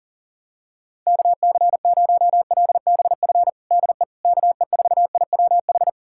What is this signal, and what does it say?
Morse code, heard as a keyed tone, about 40 wpm, 700 Hz. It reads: KC0LBF DE KE4IWH
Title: USA Callsign DE Callsign - 6 Characters - 40wpm